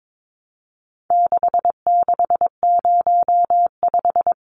Morse code, heard as a keyed tone, about 22 wpm, 700 Hz. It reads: 6605
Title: Numbers - 4-Digit 22wpm